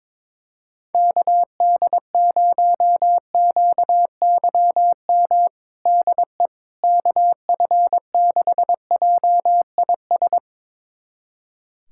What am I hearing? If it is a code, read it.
Morse code, heard as a keyed tone, about 22 wpm, 700 Hz. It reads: KD0QYM DE KF6JIS
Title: USA Callsign DE Callsign - 6 Characters - 22wpm